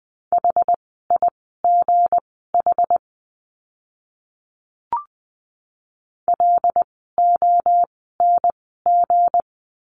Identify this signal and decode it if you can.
Morse code, heard as a keyed tone, about 20 wpm, spaced as for 15 wpm, 700 Hz. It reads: HIGH LONG
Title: 160 Lesson 26f - Letter g - Words 20wpm